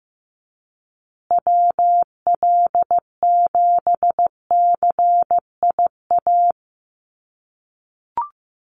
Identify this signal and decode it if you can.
Morse code, heard as a keyed tone, about 15 wpm, 700 Hz. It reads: WL7CIA
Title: USA Call Signs Repeated 2x 15wpm